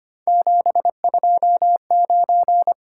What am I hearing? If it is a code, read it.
Morse code, heard as a keyed tone, about 25 wpm, 700 Hz. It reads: 729
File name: Three Letter-Number Combinations - 25wpm